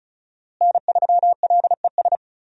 Morse code, heard as a keyed tone, about 34 wpm, 700 Hz. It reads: N3LES
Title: USA Call Signs - 5 Characters - Repeated 2x - 35wpm